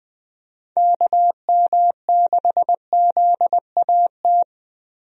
Morse code, heard as a keyed tone, about 21 wpm, 700 Hz. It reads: KM6ZAT
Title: USA Call Signs - 6 Characters - Repeated 2x - 20wpm